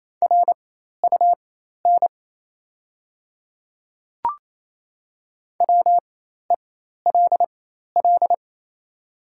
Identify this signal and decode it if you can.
Morse code, heard as a keyed tone, about 27 wpm, spaced as for 13 wpm, 700 Hz. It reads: RUN WELL
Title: 4x Character Spacing - Top 500 Words - 28wpm